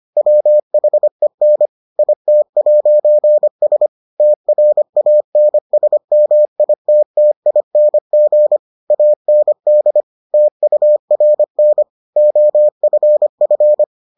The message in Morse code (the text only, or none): WHEN IT'S TRANSMITTING AND TURN OFF